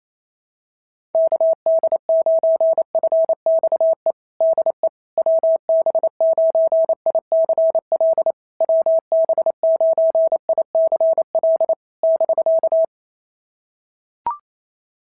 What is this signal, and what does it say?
Morse code, heard as a keyed tone, about 28 wpm, 660 Hz. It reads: KD9FXE DE WB9ICL WB9ICL <BK>
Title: SOTA QSO Practice 28wpm